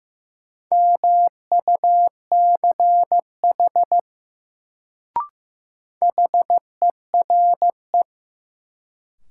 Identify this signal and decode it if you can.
Morse code, heard as a keyed tone, about 15 wpm, 700 Hz. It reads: MUCH HERE
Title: Sentences from Top 200 Words - New Word Drill - 15wpm